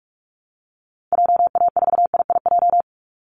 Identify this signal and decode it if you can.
Morse code, heard as a keyed tone, about 46 wpm, 700 Hz. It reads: JA4IIJ
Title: Call Signs - 45wpm